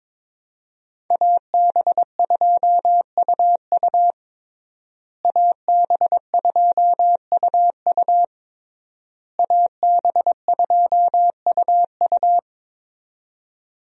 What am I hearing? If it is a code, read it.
Morse code, heard as a keyed tone, about 22 wpm, 700 Hz. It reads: AB2UU AB2UU AB2UU
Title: USA Call Signs - 5 Characters - Repeated 3x - 22wpm